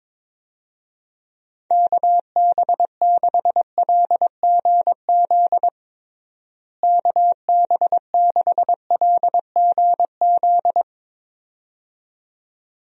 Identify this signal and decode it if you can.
Morse code, heard as a keyed tone, about 22 wpm, 700 Hz. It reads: KB6LGZ KB6LGZ
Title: USA Call Signs - 6 Characters - Repeated 2x - 22wpm